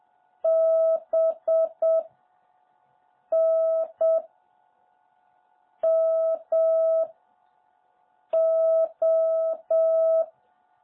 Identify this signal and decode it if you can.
Morse code, heard as a keyed tone, about 7 wpm, 650 Hz. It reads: B N M O
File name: jh8cbh / 2021cw